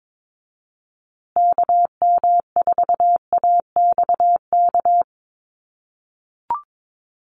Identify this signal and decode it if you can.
Morse code, heard as a keyed tone, about 22 wpm, 700 Hz. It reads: KM4AXK
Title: USA Call Signs Repeated 3x 22wpm